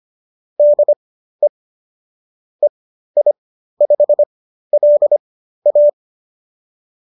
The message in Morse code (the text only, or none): DE EI5LA